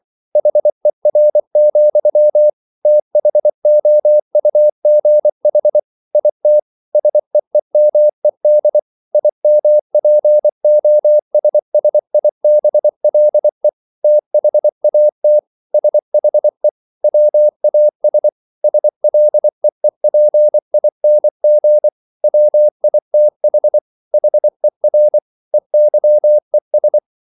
Morse code, heard as a keyed tone, about 24 wpm, 600 Hz. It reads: HER, THOUGH IT SEEMED IMPOSSIBLE THAT SHE WAS SLEEPING WITH HER EYES